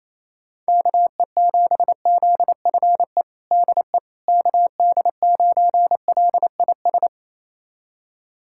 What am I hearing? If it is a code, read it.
Morse code, heard as a keyed tone, about 28 wpm, 700 Hz. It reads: KE7ZFE DE KD9LIS
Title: USA Callsign DE Callsign 28wpm